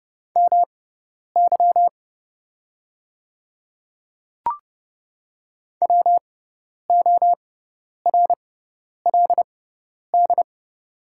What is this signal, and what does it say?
Morse code, heard as a keyed tone, about 29 wpm, spaced as for 10 wpm, 690 Hz. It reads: MY WORLD
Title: 6x Character Spacing - Top 200 Words - 30wpm